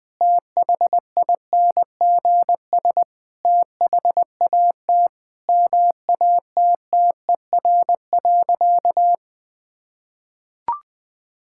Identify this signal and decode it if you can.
Morse code, heard as a keyed tone, about 20 wpm, 700 Hz. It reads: THINGS THAT MATTER.
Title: Sentences with 5-Word Phrases - 20wpm